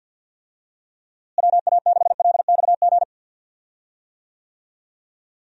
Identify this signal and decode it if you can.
Morse code, heard as a keyed tone, about 52 wpm, 700 Hz. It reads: WA6LXG
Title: USA Call Signs - 6 Characters - Repeated 3x - 50wpm